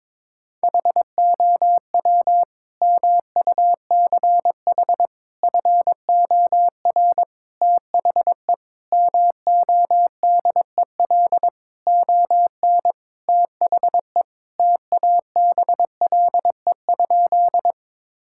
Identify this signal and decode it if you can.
Morse code, heard as a keyed tone, about 22 wpm, 700 Hz. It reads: HOW MUCH FOR THE MODEL ON THE TABLE?
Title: Sentences from Top 1000 Words - Review Part 5 - 22wpm